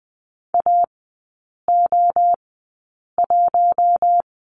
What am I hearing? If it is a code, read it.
Morse code, heard as a keyed tone, about 19 wpm, spaced as for 8 wpm, 700 Hz. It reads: AO1
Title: 067 Lesson 12e - Letter d - Three Characters 20wpm